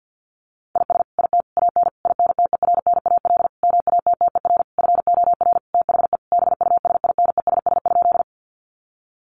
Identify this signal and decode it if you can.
Morse code, heard as a keyed tone, about 50 wpm, 700 Hz. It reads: IS IT AN INTERNAL MATTER FOR THE BUSINESS?
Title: Sentences from Top 1400 Words 50wpm